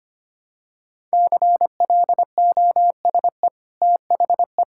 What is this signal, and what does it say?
Morse code, heard as a keyed tone, about 25 wpm, 700 Hz. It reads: CLOSE THE
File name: Easy Sentences with Hints - 25wpm